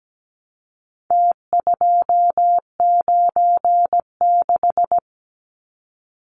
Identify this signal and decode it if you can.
Morse code, heard as a keyed tone, about 17 wpm, 700 Hz. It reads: T296
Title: Numbers - Contest Style - 4-Digit 17wpm